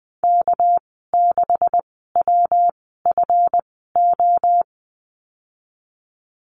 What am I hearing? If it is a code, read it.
Morse code, heard as a keyed tone, about 20 wpm, spaced as for 15 wpm, 700 Hz. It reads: K6WFO
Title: 200 Lesson 32g - Letter k - Call Signs 20wpm